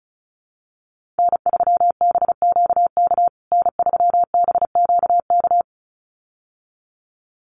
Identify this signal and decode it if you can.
Morse code, heard as a keyed tone, about 34 wpm, 700 Hz. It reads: N3BQK N3BQK
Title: USA Call Signs Repeated 2x 35wpm